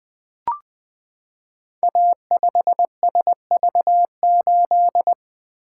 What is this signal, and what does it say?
Morse code, heard as a keyed tone, about 20 wpm, 700 Hz. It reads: A5SV8
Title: Code Groups - 5 Letter:Numbers - 20wpm